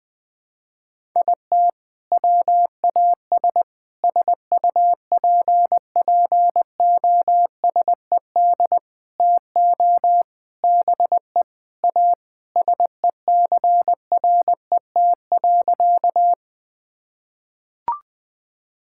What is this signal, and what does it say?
Morse code, heard as a keyed tone, about 20 wpm, 700 Hz. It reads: IT WAS SUPPOSED TO BE A SECRET.
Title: Sentences with 4-Word Phrases - Spoken Beginning - 20wpm